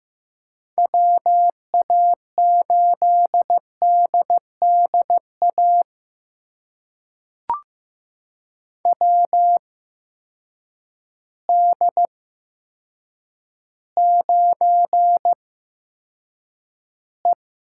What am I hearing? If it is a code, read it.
Morse code, heard as a keyed tone, about 15 wpm, spaced as for 8 wpm, 700 Hz. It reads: WA8DDA WD9E
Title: USA Call Signs - 6 Characters - 8x Character Spacing - 15wpm